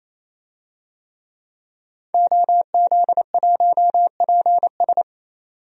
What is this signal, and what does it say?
Morse code, heard as a keyed tone, about 27 wpm, 700 Hz. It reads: OZ1PS
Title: DXCC-100 Call Signs - 5 Characters - 28wpm